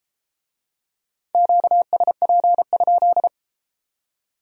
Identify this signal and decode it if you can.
Morse code, heard as a keyed tone, about 32 wpm, 700 Hz. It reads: QSP?